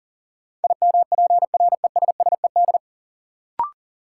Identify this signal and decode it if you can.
Morse code, heard as a keyed tone, about 37 wpm, 700 Hz. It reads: IMPRESSED
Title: Sentences from Top 1600 Words - New Word Drill - 40wpm